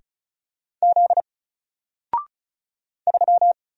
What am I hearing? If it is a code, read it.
Morse code, heard as a keyed tone, about 34 wpm, 700 Hz. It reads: Z 3
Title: Single Letter-Number 35wpm